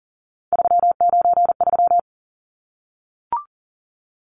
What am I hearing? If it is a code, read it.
Morse code, heard as a keyed tone, about 40 wpm, 700 Hz. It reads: 393
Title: Numbers - 3-Digit 40wpm